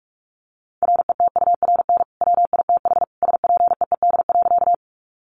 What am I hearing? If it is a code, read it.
Morse code, heard as a keyed tone, about 44 wpm, 700 Hz. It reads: RETURN WITH SPEED.
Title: Sentences from Top 800 Words 45wpm